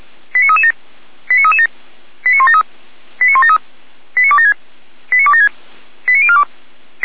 Five-tone selective call (ZVEI), heard as quick runs of five tones, tones of about 80 ms. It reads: 89337 89337 89173 89173 89276 89276 89042
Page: SDR/ZVEI